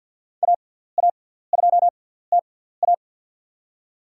Morse code, heard as a keyed tone, about 47 wpm, spaced as for 17 wpm, 700 Hz. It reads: AA2TA
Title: USA Call Signs - 5 Characters - 6x Character Spacing - 50wpm